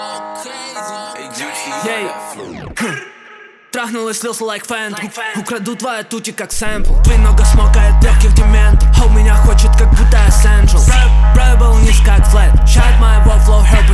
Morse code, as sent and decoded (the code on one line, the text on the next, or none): none